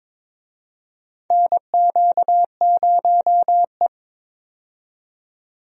Morse code, heard as a keyed tone, about 22 wpm, 700 Hz. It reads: NQ0E
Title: USA Call Signs - 4 Characters - 22wpm